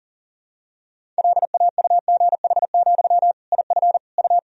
Morse code, heard as a keyed tone, about 40 wpm, 700 Hz. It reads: LAUGH, IF U
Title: Sentences from Top 500 Words and Abbreviations 40wpm